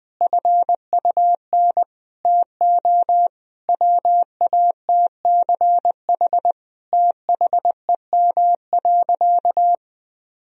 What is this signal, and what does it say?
Morse code, heard as a keyed tone, about 20 wpm, 700 Hz. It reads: FUN TO WATCH THEM.
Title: Sentences from Top 1000 Words - Review Part 4 - 20wpm